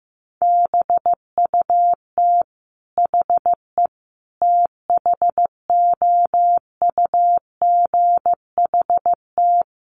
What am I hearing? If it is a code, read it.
Morse code, heard as a keyed tone, about 14 wpm, 700 Hz. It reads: BUT HE THOUGHT
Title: Sentences from Top 600 Words 15wpm